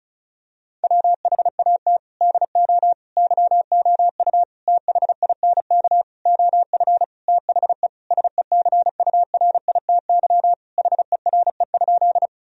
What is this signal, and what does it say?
Morse code, heard as a keyed tone, about 34 wpm, 700 Hz. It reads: WHAT DO YOU THINK OF THE SECURITY HERE?
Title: Sentences from Top 1000 Words - Review Part 4 - 35wpm